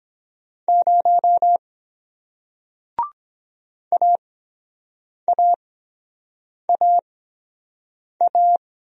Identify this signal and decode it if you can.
Morse code, heard as a keyed tone, about 23 wpm, spaced as for 7 wpm, 700 Hz. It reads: 0 AAAA